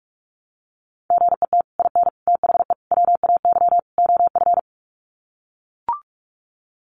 Morse code, heard as a keyed tone, about 44 wpm, 700 Hz. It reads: GET IN THE WAY OF
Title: Sets of 5 Words Encore 45wpm